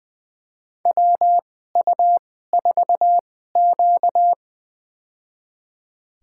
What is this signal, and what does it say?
Morse code, heard as a keyed tone, about 20 wpm, spaced as for 14 wpm, 700 Hz. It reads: WU4Q
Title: USA Call Signs - 4 Characters - 2x Character Spacing - 20wpm